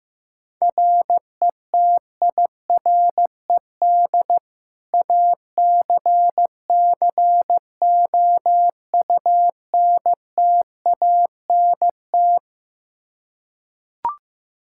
Morse code, heard as a keyed tone, about 16 wpm, 700 Hz. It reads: RETIRED ACCOUNTANT
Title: Practice QSO - Exchange 3 - More Info - 15wpm